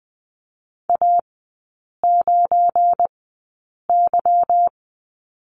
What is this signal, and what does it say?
Morse code, heard as a keyed tone, about 20 wpm, spaced as for 8 wpm, 700 Hz. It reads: A9Y
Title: 172 Lesson 28e - Number 9 - Three Characters 20wpm